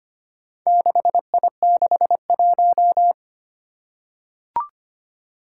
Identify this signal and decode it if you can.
Morse code, heard as a keyed tone, about 25 wpm, 700 Hz. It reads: 6I61
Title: Four Challenging Letter-Number Combinations 25wpm